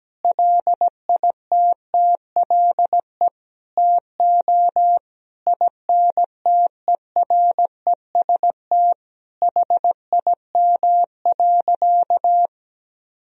Morse code, heard as a keyed tone, about 17 wpm, 700 Hz. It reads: LITTLE TO INTEREST HIM.
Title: Sentences from Top 1000 Words - Review Part 2 - 17wpm